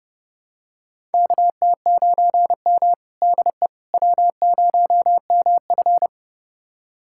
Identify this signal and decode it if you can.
Morse code, heard as a keyed tone, about 30 wpm, 700 Hz. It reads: KT9M DE W0MF
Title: USA Callsign DE Callsign - 4 Characters - 30wpm